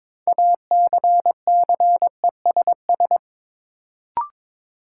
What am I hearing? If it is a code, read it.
Morse code, heard as a keyed tone, about 22 wpm, 700 Hz. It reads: ACCESS
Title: Sentences from Top 1000 Words - Review Part 4 - New Word Drill - 22wpm